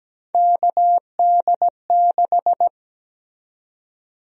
Morse code, heard as a keyed tone, about 17 wpm, 700 Hz. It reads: KD6
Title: USA Call Sign Prefixes - Repeated 2x - 17wpm